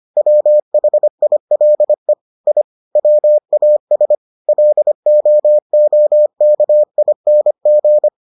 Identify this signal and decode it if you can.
Morse code, heard as a keyed tone, about 25 wpm, 600 Hz. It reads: WHILE I WAS LOOKING